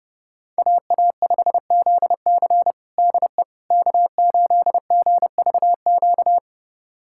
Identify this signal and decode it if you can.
Morse code, heard as a keyed tone, about 30 wpm, 700 Hz. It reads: AA5ZC DE K8GVQ
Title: USA Callsign DE Callsign - 5 Characters - 30wpm